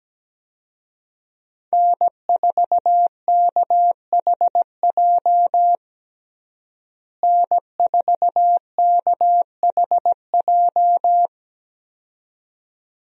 Morse code, heard as a keyed tone, about 17 wpm, 700 Hz. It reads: N4KHJ N4KHJ
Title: USA Call Signs - 5 Characters - Repeated 2x - 17wpm